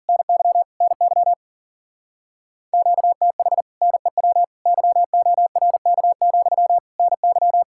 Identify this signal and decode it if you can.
Morse code, heard as a keyed tone, about 50 wpm, 700 Hz. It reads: NY NY QTH NEW YORK, NY
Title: Speed-Racing Common QSO Phrases - 40wpm